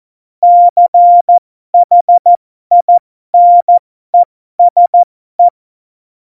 Morse code, heard as a keyed tone, about 14 wpm, spaced as for 12 wpm, 700 Hz. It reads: CHINESE
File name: Morsekurs / Witze